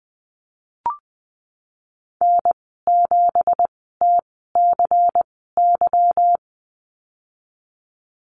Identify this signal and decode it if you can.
Morse code, heard as a keyed tone, about 20 wpm, spaced as for 15 wpm, 700 Hz. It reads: N7TCY
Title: 167 Lesson 27g - Number 7 - Call Signs 20wpm